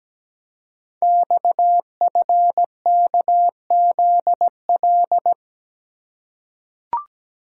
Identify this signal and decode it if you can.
Morse code, heard as a keyed tone, about 17 wpm, 700 Hz. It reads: XFKZL
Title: Code Groups - 5 Letters - 17wpm